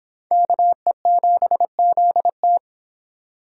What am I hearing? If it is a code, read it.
Morse code, heard as a keyed tone, about 25 wpm, 700 Hz. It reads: KE7ZT